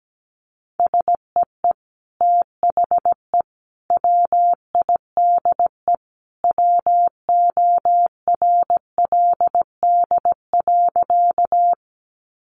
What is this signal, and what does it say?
Morse code, heard as a keyed tone, about 17 wpm, 700 Hz. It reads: SEE THE WIDE WORLD.
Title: Sentences from Top 900 Words 17wpm